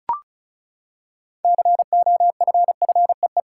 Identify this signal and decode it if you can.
Morse code, heard as a keyed tone, about 36 wpm, 700 Hz. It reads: COFFEE
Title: Six Letter Words - 35wpm